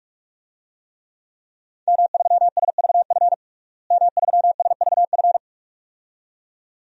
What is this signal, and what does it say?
Morse code, heard as a keyed tone, about 46 wpm, 700 Hz. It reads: M3SVF M3SVF
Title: DXCC-100 Call Signs - 5 Characters - Repeated 2x - 45wpm